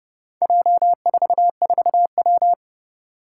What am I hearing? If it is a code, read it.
Morse code, heard as a keyed tone, about 30 wpm, 700 Hz. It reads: J44W
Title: Four Challenging Letter-Number Combinations 30wpm